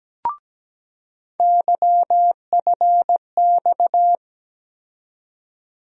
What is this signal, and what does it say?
Morse code, heard as a keyed tone, about 17 wpm, 700 Hz. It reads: YFX
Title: USA Call Sign Postfixes - 3 Letters 17wpm